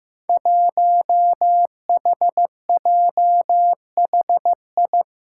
Morse code, heard as a keyed tone, about 15 wpm, 700 Hz. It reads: 1HJHI
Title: Five Challenging Letter-Number Combinations 15wpm